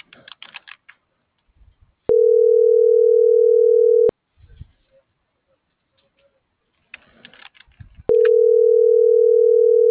Ringback tone (North America), 440 + 480 Hz.